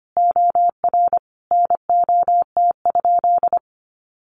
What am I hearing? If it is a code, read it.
Morse code, heard as a keyed tone, about 25 wpm, 700 Hz. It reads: OR NOT?
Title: Sentences from Top 1000 Words 25wpm